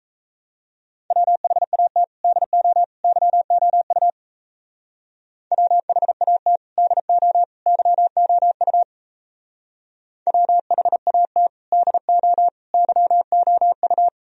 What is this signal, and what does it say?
Morse code, heard as a keyed tone, about 36 wpm, 700 Hz. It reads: WHAT DO YOU WHAT DO YOU WHAT DO YOU